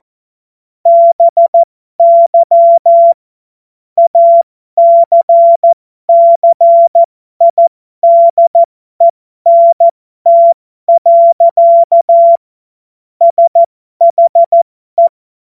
Morse code, heard as a keyed tone, about 14 wpm, spaced as for 12 wpm, 680 Hz. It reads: BY ACCIDENT. SHE
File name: Morsekurs / Witze